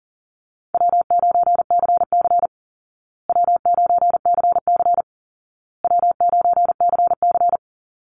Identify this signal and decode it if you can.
Morse code, heard as a keyed tone, about 40 wpm, 700 Hz. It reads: W9CC W9CC W9CC
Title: Call Signs Repeated 2x - 40wpm